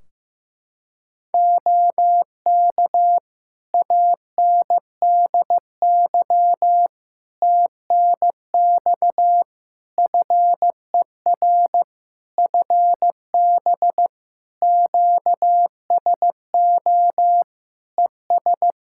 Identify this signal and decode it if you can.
Morse code, heard as a keyed tone, about 15 wpm, 700 Hz. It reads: OK ANDY TNX FER FB QSO ES
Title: K1USN SST QSO Practice 15wpm